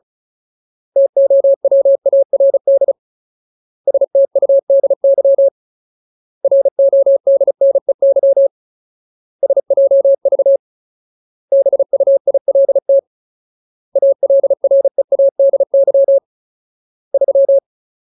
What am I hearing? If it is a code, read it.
Morse code, heard as a keyed tone, about 34 wpm, 550 Hz. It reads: TOWARD STUDY RODNEY SJV BUILT ALREADY 3
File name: Ultimate Word List / 35WPM Ultimate